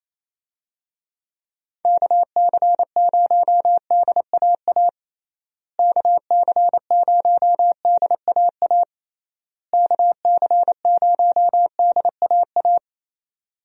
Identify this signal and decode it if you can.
Morse code, heard as a keyed tone, about 28 wpm, 700 Hz. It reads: KC0DAA KC0DAA KC0DAA
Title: USA Call Signs - 6 Characters - Repeated 3x - 28wpm